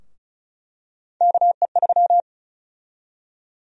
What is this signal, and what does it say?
Morse code, heard as a keyed tone, about 36 wpm, 700 Hz. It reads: KE3
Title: USA Call Sign Prefixes - 35wpm